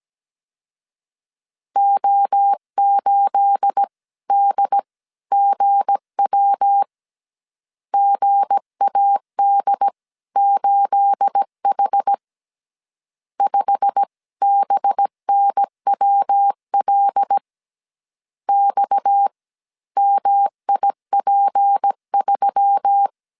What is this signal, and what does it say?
Morse code, heard as a keyed tone, about 17 wpm, spaced as for 14 wpm, 800 Hz. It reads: O8DGW GAD8H 5BNWL XMIP3
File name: CW / DET / 1 9 1500C S C 16wpm 2909